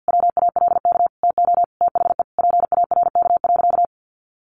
Morse code, heard as a keyed tone, about 48 wpm, 700 Hz. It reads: WALK TO THE PARK.
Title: Sentences from Top 1400 Words 50wpm